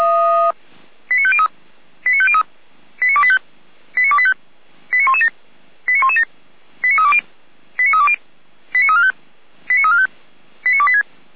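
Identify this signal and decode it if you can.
Five-tone selective call (ZVEI), heard as quick runs of five tones, tones of about 70 ms. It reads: 89503 89503 89285 89285 89117 89117 89320 89320 89345 89345 89286
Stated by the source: ZVEI five-tone call received by SDR, from the feuerwehr-orlamuende.org fire brigade audio server.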